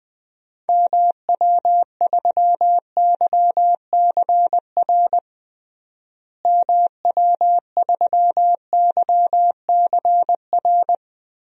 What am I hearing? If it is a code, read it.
Morse code, heard as a keyed tone, about 20 wpm, 700 Hz. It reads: MW3YCR MW3YCR
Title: DXCC-100 Call Signs - 6 Characters - Repeated 3x - 20wpm